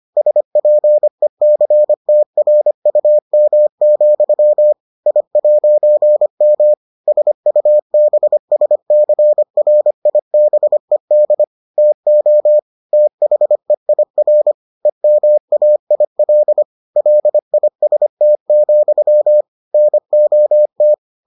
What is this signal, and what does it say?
Morse code, heard as a keyed tone, about 25 wpm, 600 Hz. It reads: SPECTRUM, I'M SUBSCRIBED TO THEIR EMAIL LIST, NOT